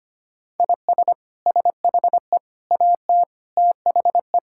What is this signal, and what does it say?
Morse code, heard as a keyed tone, about 25 wpm, 700 Hz. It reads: IS SHE AT THE
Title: Sentences from Top 1000 Words - Review Part 5 - 25wpm